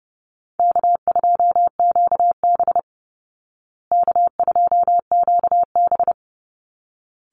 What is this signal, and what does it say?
Morse code, heard as a keyed tone, about 30 wpm, 700 Hz. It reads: K2QB K2QB
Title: Easy Call Signs Repeated 2x - 30wpm